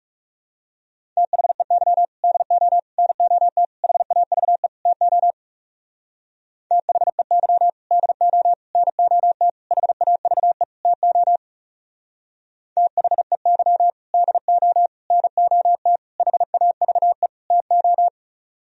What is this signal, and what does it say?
Morse code, heard as a keyed tone, about 37 wpm, 700 Hz. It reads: THEY DO NOT HAVE TO THEY DO NOT HAVE TO THEY DO NOT HAVE TO
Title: Speed-Racing - Sets of 5 Words 30wpm